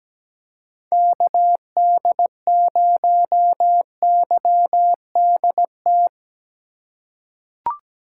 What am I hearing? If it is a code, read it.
Morse code, heard as a keyed tone, about 17 wpm, 700 Hz. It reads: KD0YDT
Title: USA Call Signs - 6 Characters - 2x Character Spacing - 17wpm